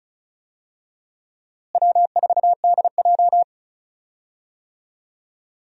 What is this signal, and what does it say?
Morse code, heard as a keyed tone, about 36 wpm, 700 Hz. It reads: W4DJ
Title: USA Call Signs - 4 Characters - Repeated 3x - 35wpm